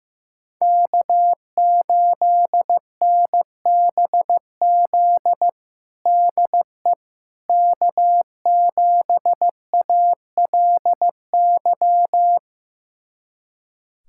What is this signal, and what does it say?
Morse code, heard as a keyed tone, about 15 wpm, 700 Hz. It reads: K8NBZ DE K7ALY
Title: USA Callsign DE Callsign - 5 Characters - 15wpm